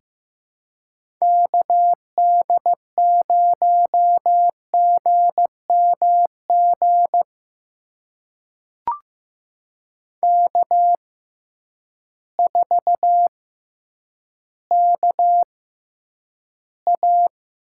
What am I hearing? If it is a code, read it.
Morse code, heard as a keyed tone, about 15 wpm, spaced as for 9 wpm, 700 Hz. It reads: KD0GMG K4KA
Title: USA Call Signs - 6x Character Spacing - 15wpm